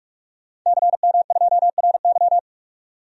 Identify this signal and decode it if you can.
Morse code, heard as a keyed tone, about 43 wpm, 700 Hz. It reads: CM2RY
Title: DXCC-100 Call Signs - 5 Characters - 45wpm